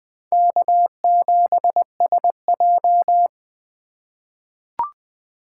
Morse code, heard as a keyed tone, about 20 wpm, 700 Hz. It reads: K7SJ
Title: USA Call Signs - 4 Characters - 3x Character Spacing - 20wpm